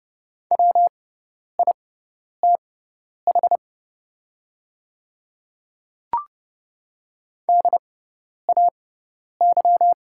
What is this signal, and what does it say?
Morse code, heard as a keyed tone, about 30 wpm, spaced as for 10 wpm, 700 Hz. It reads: WITH DAY
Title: 6x Character Spacing - Top 100 Words - 30wpm